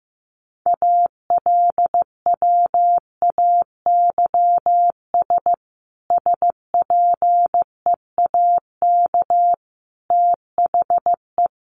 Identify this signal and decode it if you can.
Morse code, heard as a keyed tone, about 15 wpm, 700 Hz. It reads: ALWAYS SPEAK THE
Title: Sentences from Top 1100 Words 15wpm